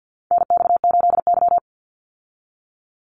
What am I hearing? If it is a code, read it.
Morse code, heard as a keyed tone, about 48 wpm, 700 Hz. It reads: NX8Y
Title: Call Signs Repeated 3x - 50wpm